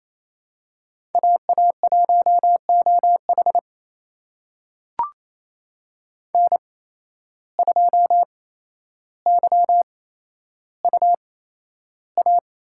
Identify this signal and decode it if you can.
Morse code, heard as a keyed tone, about 28 wpm, spaced as for 12 wpm, 700 Hz. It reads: AA1OH N2YUA
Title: USA Call Signs - 5 Characters - 8x Character Spacing - 28wpm